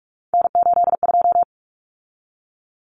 Morse code, heard as a keyed tone, about 44 wpm, 700 Hz. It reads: N82
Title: Common Cut Numbers - 3-Digit 45wpm